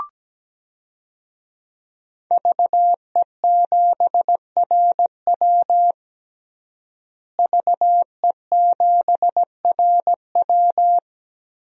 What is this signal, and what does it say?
Morse code, heard as a keyed tone, about 17 wpm, 700 Hz. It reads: VE7RW VE7RW
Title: DXCC-100 Call Signs - 5 Characters - Repeated 3x - 17wpm